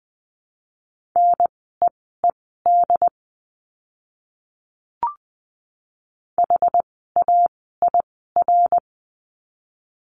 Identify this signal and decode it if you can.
Morse code, heard as a keyed tone, about 20 wpm, spaced as for 14 wpm, 700 Hz. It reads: NEED HAIR
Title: 101 Lesson 17f - Letter u - Words 20wpm